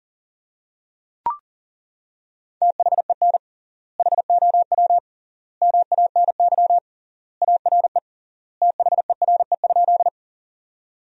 Morse code, heard as a keyed tone, about 40 wpm, 700 Hz. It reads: THEN HOW MANY ARE THERE?
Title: 3x Word-Spacing - Sentences from Top 100 Words - 40wpm